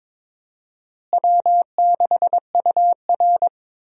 Morse code, heard as a keyed tone, about 22 wpm, 700 Hz. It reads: W6UR
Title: USA Call Signs - 4 Characters - Repeated 3x - 22wpm